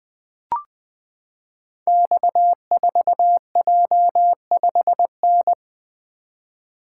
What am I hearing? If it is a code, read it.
Morse code, heard as a keyed tone, about 21 wpm, 700 Hz. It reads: X4J5N
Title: Code Groups - 5 Letter:Numbers - 20wpm